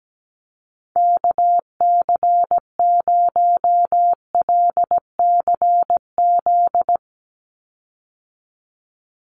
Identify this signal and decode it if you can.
Morse code, heard as a keyed tone, about 17 wpm, 700 Hz. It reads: KC0LCZ
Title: USA Call Signs 17wpm